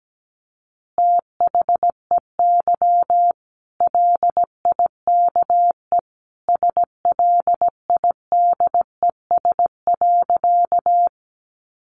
Morse code, heard as a keyed tone, about 17 wpm, 700 Hz. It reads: THEY LIKE SLIDES.
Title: Easy Phrases 17wpm